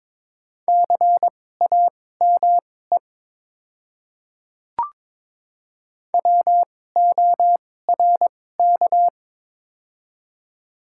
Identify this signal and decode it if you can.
Morse code, heard as a keyed tone, about 22 wpm, spaced as for 16 wpm, 700 Hz. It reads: CAME WORK
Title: 2x Character Spacing - Top 300 Words - 22wpm